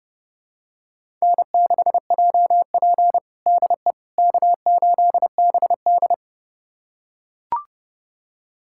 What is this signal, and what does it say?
Morse code, heard as a keyed tone, about 32 wpm, 700 Hz. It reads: N6JP DE K8BD
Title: USA Callsign DE Callsign - 4 Characters - 30wpm